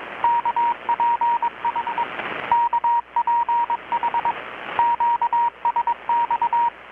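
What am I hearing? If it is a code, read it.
Morse code, heard as a keyed tone, about 22 wpm, 940 Hz, with radio noise. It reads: KPH KPH QSX
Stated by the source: Morse from KPH's Night of Nights XX maritime radiotelegraphy event, received in Chiapas, Mexico about 3,800 km away on a Kenwood R-600 with a grounded 10 m long-wire antenna.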